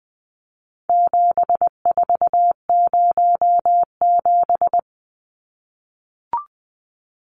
Morse code, heard as a keyed tone, about 20 wpm, 700 Hz. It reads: 7407
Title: Numbers - 4-Digit 20wpm